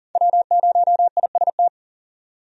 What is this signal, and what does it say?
Morse code, heard as a keyed tone, about 40 wpm, 700 Hz. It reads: W0IST
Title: USA Call Signs - 5 Characters - 40wpm